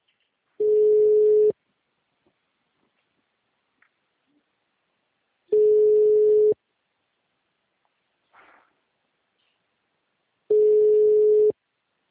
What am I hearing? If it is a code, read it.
Ringback tone (Europe, 425 Hz).
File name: out- -2004- - - .43